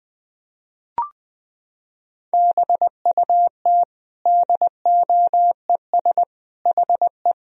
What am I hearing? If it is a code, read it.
Morse code, heard as a keyed tone, about 20 wpm, 700 Hz. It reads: BUT DOES HE
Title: Sentences from Top 1000 Words - Review Part 1 - 20wpm